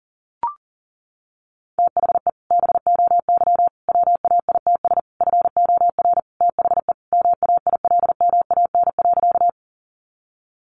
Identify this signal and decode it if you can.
Morse code, heard as a keyed tone, about 40 wpm, 700 Hz. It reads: THE BOY WAITS FOR THE MAILMAN.
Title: Easy Phrases 40wpm